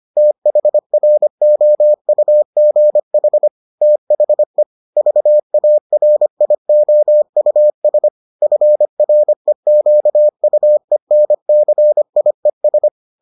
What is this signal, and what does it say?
Morse code, heard as a keyed tone, about 25 wpm, 600 Hz. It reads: THROUGH THE VARIOUS FREQUENCIES